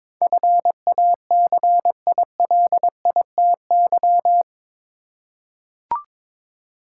Morse code, heard as a keyed tone, about 22 wpm, 700 Hz. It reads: FACILITY
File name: practice / 22wpm / Eight Letter Words - 22wpm